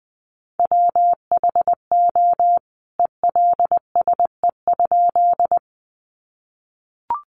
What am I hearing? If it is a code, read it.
Morse code, heard as a keyed tone, about 20 wpm, 700 Hz. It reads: WHO ELSE?
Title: A Sound Of Thunder 20wpm